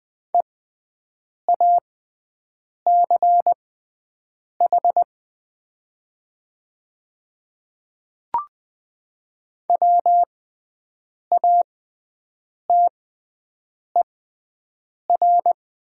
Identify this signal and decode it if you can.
Morse code, heard as a keyed tone, about 20 wpm, spaced as for 7 wpm, 700 Hz. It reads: EACH WATER
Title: 6x Character Spacing - Top 100 Words - 20wpm